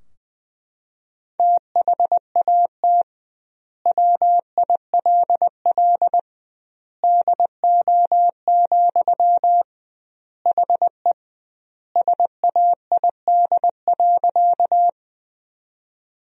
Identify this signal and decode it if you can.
Morse code, heard as a keyed tone, about 20 wpm, 700 Hz. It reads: THAT WILL DO, HE SAID.
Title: 2x Word-Spacing - Sentences from Top 100 Words - 20wpm